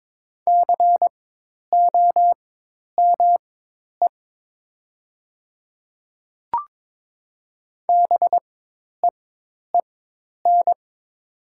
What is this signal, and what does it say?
Morse code, heard as a keyed tone, about 22 wpm, spaced as for 10 wpm, 700 Hz. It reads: COME BEEN
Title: 4x Character Spacing - Top 100 Words - 22wpm